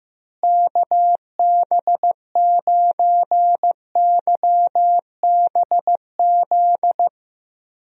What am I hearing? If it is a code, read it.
Morse code, heard as a keyed tone, about 15 wpm, 700 Hz. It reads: KB9YBZ
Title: USA Call Signs - 6 Characters - Repeated 3x - 15wpm